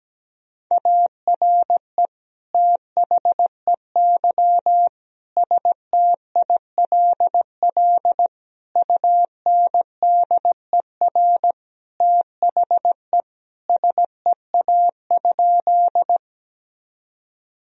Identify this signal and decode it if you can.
Morse code, heard as a keyed tone, about 17 wpm, 700 Hz. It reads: ARE THEY STILL UNDER THE SEA?
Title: Sentences from Top 1000 Words - Review Part 1 - 17wpm